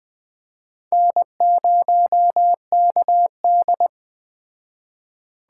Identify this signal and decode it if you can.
Morse code, heard as a keyed tone, about 20 wpm, 700 Hz. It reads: N0KD